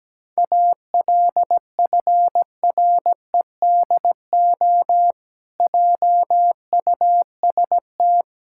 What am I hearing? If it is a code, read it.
Morse code, heard as a keyed tone, about 17 wpm, 700 Hz. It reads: ALFREDO JUST
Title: Pangram Challenge v2 - 17wpm